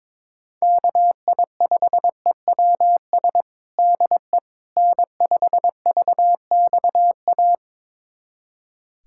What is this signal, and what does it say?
Morse code, heard as a keyed tone, about 22 wpm, 700 Hz. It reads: KI5EWS DE N5VXA
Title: USA Callsign DE Callsign 22wpm